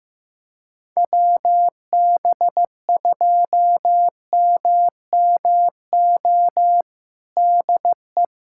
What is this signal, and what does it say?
Morse code, heard as a keyed tone, about 15 wpm, 700 Hz. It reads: WB2MMO DE
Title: USA Callsign DE Callsign - 6 Characters - 15wpm